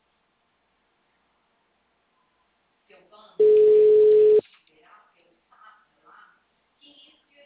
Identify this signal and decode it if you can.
Ringback tone (Europe, 425 Hz), with speech.